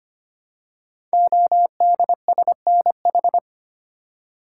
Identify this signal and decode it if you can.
Morse code, heard as a keyed tone, about 25 wpm, 700 Hz. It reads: ODSNH
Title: Code Groups - 5 Letter:Numbers - 25wpm